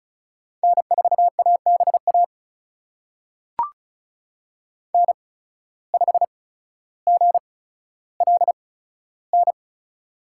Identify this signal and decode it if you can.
Morse code, heard as a keyed tone, about 34 wpm, spaced as for 15 wpm, 700 Hz. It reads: N4ABA N5GLN
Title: USA Call Signs - 8x Character Spacing - 35wpm